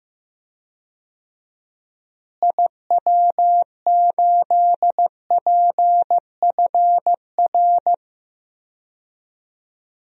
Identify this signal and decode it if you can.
Morse code, heard as a keyed tone, about 15 wpm, 700 Hz. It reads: IW8PFR
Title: DXCC-100 Call Signs - 6 Characters - Repeated 3x - 15wpm